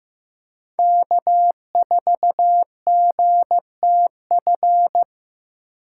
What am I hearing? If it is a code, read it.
Morse code, heard as a keyed tone, about 15 wpm, 700 Hz. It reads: K4GTF
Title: USA Call Signs - 5 Characters - 2x Character Spacing - 15wpm